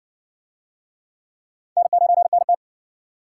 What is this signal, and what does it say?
Morse code, heard as a keyed tone, about 60 wpm, 700 Hz. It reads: N9NT